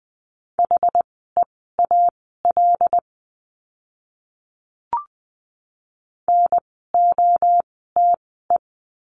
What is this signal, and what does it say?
Morse code, heard as a keyed tone, about 20 wpm, spaced as for 14 wpm, 700 Hz. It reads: HEAL NOTE
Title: 075 Lesson 13f - Letter L - Words 20wpm